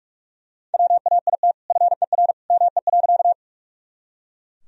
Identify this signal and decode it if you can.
Morse code, heard as a keyed tone, about 44 wpm, 700 Hz. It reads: WAIT FER ME.
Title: Sentences from Top 500 Words and Abbreviations 45wpm